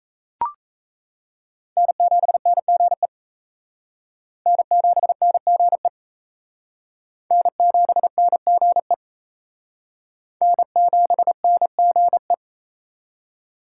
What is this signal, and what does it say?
Morse code, heard as a keyed tone, about 34 wpm, 700 Hz. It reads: N7NGE N7NGE N7NGE N7NGE
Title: Speed Racing - USA 1x3 Call Signs 28wpm